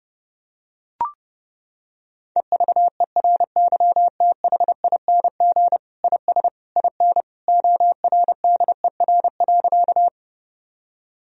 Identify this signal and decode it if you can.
Morse code, heard as a keyed tone, about 30 wpm, 700 Hz. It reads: EVERYTHING IS IN ORDER.
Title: Sentences from Top 1000 Words - Review Part 2 - 30wpm